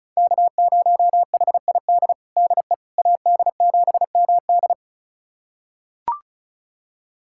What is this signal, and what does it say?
Morse code, heard as a keyed tone, about 36 wpm, 700 Hz. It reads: K0HID DE AD7MD
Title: USA Callsign DE Callsign - 5 Characters - 35wpm